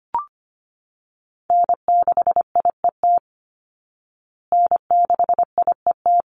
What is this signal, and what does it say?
Morse code, heard as a keyed tone, about 25 wpm, 700 Hz. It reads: N6IET N6IET
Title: Call Signs Repeated 2x - 25wpm